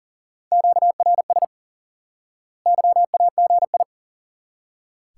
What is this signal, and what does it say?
Morse code, heard as a keyed tone, about 37 wpm, 700 Hz. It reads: QRS YAGI
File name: IQE Course 3 - Lesson 89 - YAGI - Two Elements - 40wpm